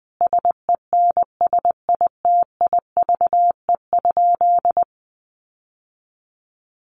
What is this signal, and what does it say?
Morse code, heard as a keyed tone, about 20 wpm, 700 Hz. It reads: SENSITIVE?
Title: Sentences from Top 1400 Words 20wpm